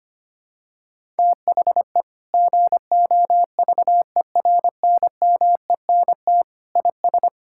Morse code, heard as a keyed tone, about 25 wpm, 700 Hz. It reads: THE GOVERNMENT IS
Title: Sentences from Top 1000 Words - Review Part 5 - 25wpm